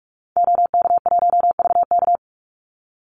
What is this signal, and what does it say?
Morse code, heard as a keyed tone, about 44 wpm, 700 Hz. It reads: OK1VK
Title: Call Signs - 45wpm